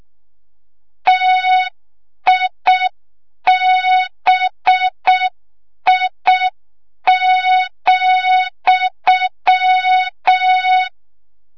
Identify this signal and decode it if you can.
Morse code, heard as a keyed tone, about 6 wpm, 3700 Hz, with noise. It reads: TIBI,
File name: Morze-HA2MN / hang / 25 GYAKORLAT